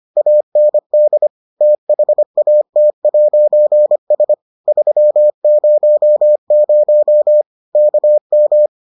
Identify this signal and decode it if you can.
Morse code, heard as a keyed tone, about 25 wpm, 600 Hz. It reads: AND THAT'S 300 KM